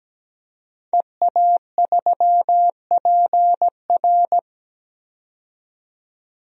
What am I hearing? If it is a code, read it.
Morse code, heard as a keyed tone, about 17 wpm, 700 Hz. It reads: EA3PR
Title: DXCC-100 Call Signs - 5 Characters - 17wpm